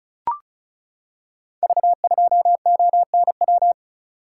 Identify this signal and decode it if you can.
Morse code, heard as a keyed tone, about 34 wpm, 700 Hz. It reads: V2ONW
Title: Code Groups - 5 Letter:Numbers - 35wpm